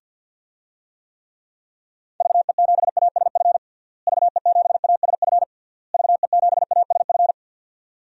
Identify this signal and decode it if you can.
Morse code, heard as a keyed tone, about 49 wpm, 700 Hz. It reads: VE7ASF VE7ASF VE7ASF
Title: DXCC-100 Call Signs - 6 Characters - Repeated 3x - 50wpm